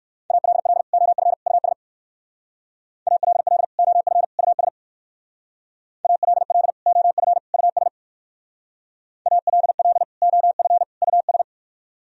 Morse code, heard as a keyed tone, about 56 wpm, 700 Hz. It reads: ALL OF US ALL OF US ALL OF US ALL OF US
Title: Speed-Racing - Sets of 3 Words 45wpm